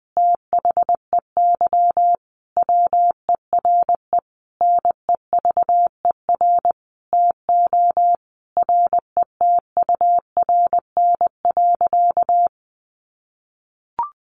Morse code, heard as a keyed tone, about 20 wpm, 700 Hz. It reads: THEY WERE NEVER TO RETURN.
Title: Sentences from Top 800 Words 20wpm